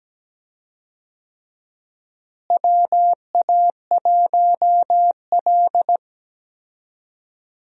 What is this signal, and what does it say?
Morse code, heard as a keyed tone, about 17 wpm, 700 Hz. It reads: WA1L
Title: USA Call Signs - 4 Characters - Repeated 3x - 17wpm